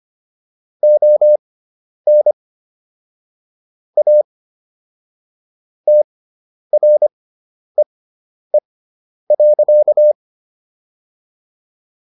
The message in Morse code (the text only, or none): ON A TREE.